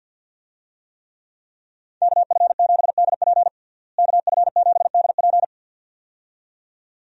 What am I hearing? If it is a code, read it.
Morse code, heard as a keyed tone, about 46 wpm, 700 Hz. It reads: KF7DP KF7DP
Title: USA Call Signs - 5 Characters - Repeated 2x - 50wpm